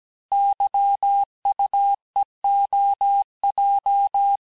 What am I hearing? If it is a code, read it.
Morse code, heard as a keyed tone, about 17 wpm, 800 Hz. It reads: YUEOJ
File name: cours-du-lundi files / cours-1587-texte-chiffre-vitesse-16MPM